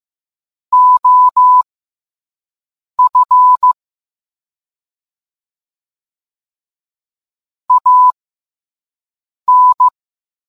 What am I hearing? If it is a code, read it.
Morse code, heard as a keyed tone, about 15 wpm, spaced as for 5 wpm, 1000 Hz. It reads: OF AN